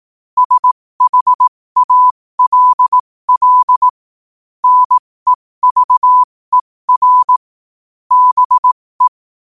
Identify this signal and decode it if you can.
Morse code, heard as a keyed tone, about 18 wpm, spaced as for 15 wpm, 1000 Hz. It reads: SHALL NEVER BE
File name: qotdfiles / 15wpm